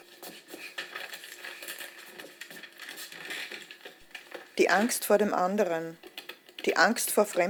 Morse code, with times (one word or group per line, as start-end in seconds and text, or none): none